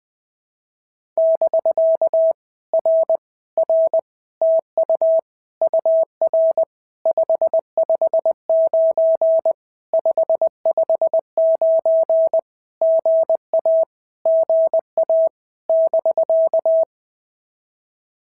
Morse code, heard as a keyed tone, about 20 wpm, 650 Hz. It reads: <BK> R R TU UR 559 559 GA GA <BK>
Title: POTA QSO Practice - 20wpm